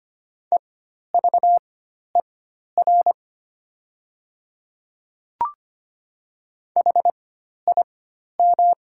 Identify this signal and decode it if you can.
Morse code, heard as a keyed tone, about 25 wpm, spaced as for 12 wpm, 700 Hz. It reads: EVER HIM
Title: 4x Character Spacing - Top 500 Words - 25wpm